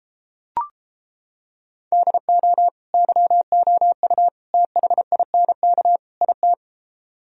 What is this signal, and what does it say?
Morse code, heard as a keyed tone, about 31 wpm, 700 Hz. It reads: DO YOU THINK IT